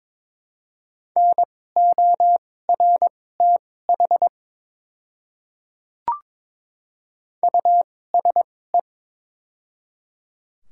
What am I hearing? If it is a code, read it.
Morse code, heard as a keyed tone, about 22 wpm, spaced as for 16 wpm, 700 Hz. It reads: NORTH USE
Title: 2x Character Spacing - Top 300 Words - 22wpm